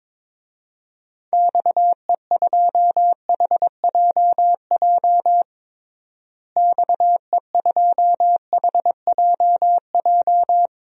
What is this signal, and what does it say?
Morse code, heard as a keyed tone, about 22 wpm, 700 Hz. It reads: XE2HJJ XE2HJJ
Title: DXCC-100 Call Signs - 6 Characters - Repeated 3x - 22wpm